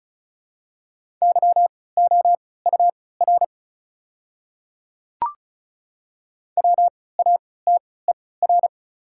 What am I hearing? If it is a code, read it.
Morse code, heard as a keyed tone, about 34 wpm, spaced as for 20 wpm, 700 Hz. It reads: YOUR WATER
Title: 3x Character Spacing - Top 200 Words - 35wpm